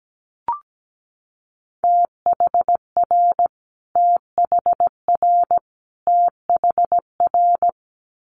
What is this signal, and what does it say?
Morse code, heard as a keyed tone, about 17 wpm, 700 Hz. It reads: THR THR THR
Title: Multi-letter Phonemes Repeated 3x 17wpm